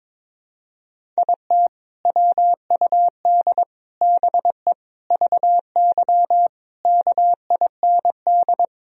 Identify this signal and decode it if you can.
Morse code, heard as a keyed tone, about 22 wpm, 700 Hz. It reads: IT WUD BE VY KIND
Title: Sentences from Top 200 Words and Abbreviations 22wpm